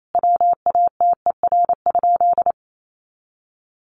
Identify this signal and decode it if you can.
Morse code, heard as a keyed tone, about 28 wpm, 700 Hz. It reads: WATER?
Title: Sentences from Top 200 Words 28wpm